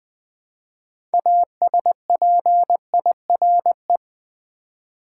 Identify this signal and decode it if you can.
Morse code, heard as a keyed tone, about 20 wpm, 700 Hz. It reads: ASPIRE
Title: 6 Letter Words - First 3 Letters Given - 20wpm